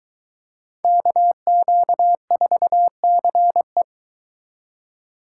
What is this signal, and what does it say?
Morse code, heard as a keyed tone, about 23 wpm, 700 Hz. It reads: KQ4CE